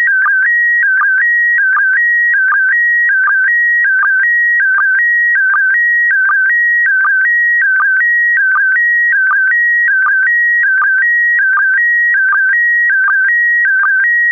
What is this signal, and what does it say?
SSTV (slow-scan television), PD 180.
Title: Data broadcast of 27 July 2020, SSTV and Fldigi